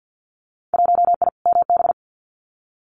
Morse code, heard as a keyed tone, about 50 wpm, 700 Hz. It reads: 1I MB
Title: Winter Field Day Exchanges Encore 50wpm